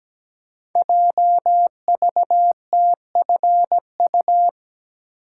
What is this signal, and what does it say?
Morse code, heard as a keyed tone, about 17 wpm, 700 Hz. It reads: JVTFU
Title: Code Groups - 5 Letter:Numbers - 17wpm